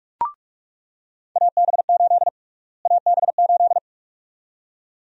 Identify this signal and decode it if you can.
Morse code, heard as a keyed tone, about 46 wpm, 700 Hz. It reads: AB8 AB8
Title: USA Call Sign Prefixes - Repeated 2x - 45wpm